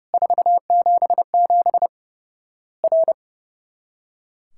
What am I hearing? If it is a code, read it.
Morse code, heard as a keyed tone, about 30 wpm, 700 Hz. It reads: 477 R
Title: Worked All Europe DX Contest - QTC Exchange - 30wpm